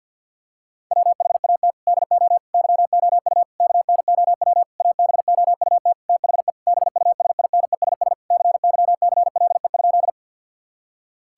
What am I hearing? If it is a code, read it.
Morse code, heard as a keyed tone, about 52 wpm, 700 Hz. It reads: WHAT DO YOU KNOW ABOUT THE BUSINESS CYCLE?
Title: Sentences from Top 1000 Words - Review Part 6 - 50wpm